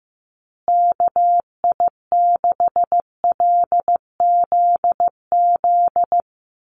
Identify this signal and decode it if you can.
Morse code, heard as a keyed tone, about 15 wpm, 700 Hz. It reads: KI6LZZ
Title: USA Call Signs Repeated 3x 15wpm